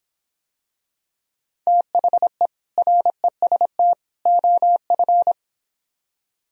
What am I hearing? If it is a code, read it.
Morse code, heard as a keyed tone, about 26 wpm, 700 Hz. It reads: THE REST OF